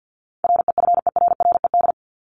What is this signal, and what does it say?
Morse code, heard as a keyed tone, about 50 wpm, 700 Hz. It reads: REFERRED
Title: ED Suffix - Top 100 Words with Base Words 50wpm